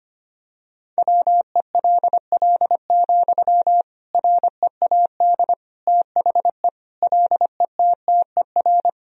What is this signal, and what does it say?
Morse code, heard as a keyed tone, about 25 wpm, 700 Hz. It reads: WELL, READ THE LETTER
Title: Sentences from Top 300 Words and Abbreviations 25wpm